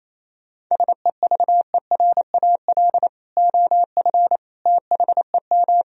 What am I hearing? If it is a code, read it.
Morse code, heard as a keyed tone, about 28 wpm, 700 Hz. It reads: SEVERAL OF THEM
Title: Sentences from Top 1000 Words - Review Part 2 - 28wpm